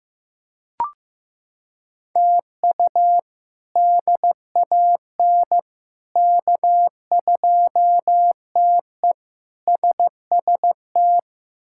Morse code, heard as a keyed tone, about 15 wpm, 700 Hz. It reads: TU DAN K2TE SST
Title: K1USN SST QSO Practice - 15wpm